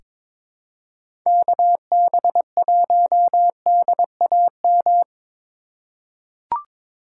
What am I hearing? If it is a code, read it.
Morse code, heard as a keyed tone, about 22 wpm, 700 Hz. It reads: KB1DAM
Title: USA Call Signs - 6 Characters - 2x Character Spacing - 22wpm